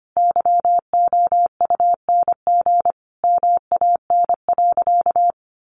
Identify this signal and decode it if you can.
Morse code, heard as a keyed tone, about 25 wpm, 700 Hz. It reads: YOUNG MAN.
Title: Sentences from Top 1000 Words 25wpm